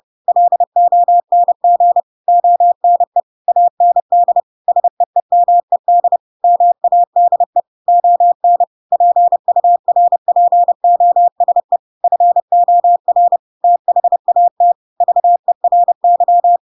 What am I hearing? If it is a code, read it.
Morse code, heard as a keyed tone, about 30 wpm, 700 Hz. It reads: LONG ONE AND SEEMED MADE ON PURPOSE FOR THAT VERY
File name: Pinocchio / Pinocchio ch3 30 30 ik1hge